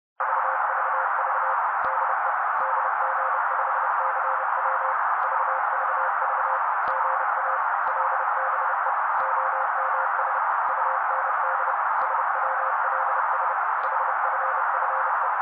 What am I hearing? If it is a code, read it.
Morse code, heard as a keyed tone, about 30 wpm, 530 Hz. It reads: VVV DE DG5CW VVV GA LBE OMS UND SWLS HFFE
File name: daily / 29wpm noise